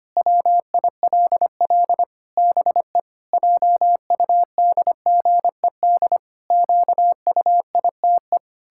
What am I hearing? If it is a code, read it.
Morse code, heard as a keyed tone, about 25 wpm, 700 Hz. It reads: WILL BE JUDGED QUITE
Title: Pangram Challenge v2 - 25wpm